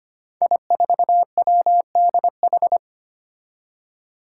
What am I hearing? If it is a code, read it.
Morse code, heard as a keyed tone, about 25 wpm, 700 Hz. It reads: I4WDH
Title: Five Challenging Letter-Number Combinations 25wpm